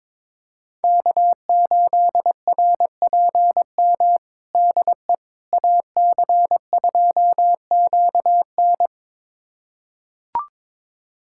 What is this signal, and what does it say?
Morse code, heard as a keyed tone, about 22 wpm, 700 Hz. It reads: K8RPM DE AC2QN
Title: USA Callsign DE Callsign - 5 Characters - 22wpm